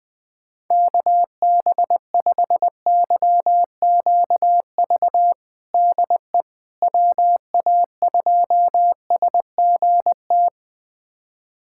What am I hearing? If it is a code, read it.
Morse code, heard as a keyed tone, about 20 wpm, 700 Hz. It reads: KB5YQV DE WA2SGT
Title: USA Callsign DE Callsign 20wpm